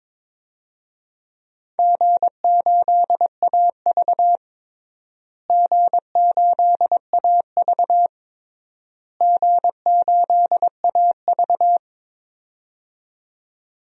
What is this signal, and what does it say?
Morse code, heard as a keyed tone, about 22 wpm, 700 Hz. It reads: G8AV G8AV G8AV
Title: DXCC-100 Call Signs - 4 Characters - Repeated 3x - 22wpm